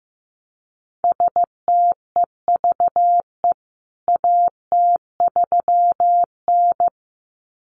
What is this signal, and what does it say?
Morse code, heard as a keyed tone, about 15 wpm, 700 Hz. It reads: STEVE AT3N
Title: CWT Exchange 15wpm